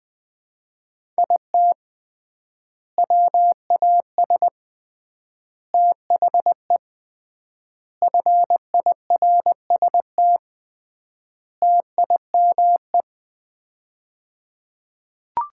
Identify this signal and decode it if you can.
Morse code, heard as a keyed tone, about 20 wpm, 700 Hz. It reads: IT WAS THE FIRST TIME
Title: 3x Word-Spacing - Sets of 5 Words - 20wpm